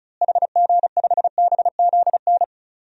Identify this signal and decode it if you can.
Morse code, heard as a keyed tone, about 36 wpm, 700 Hz. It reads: HG5BZN
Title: DXCC-100 Call Signs - 6 Characters - Repeated 3x - 35wpm